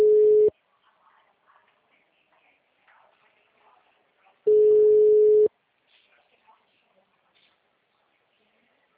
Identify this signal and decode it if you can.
Ringback tone (Europe, 425 Hz).